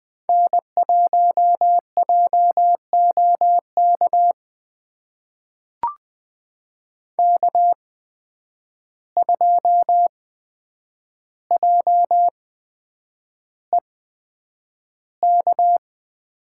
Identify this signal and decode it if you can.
Morse code, heard as a keyed tone, about 20 wpm, spaced as for 9 wpm, 700 Hz. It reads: N1JOK K2JEK
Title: USA Call Signs - 5 Characters - 8x Character Spacing - 20wpm